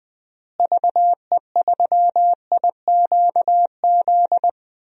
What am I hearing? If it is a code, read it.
Morse code, heard as a keyed tone, about 20 wpm, 700 Hz. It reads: VE3IQZ
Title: DXCC-100 Call Signs - 6 Characters - Repeated 3x - 20wpm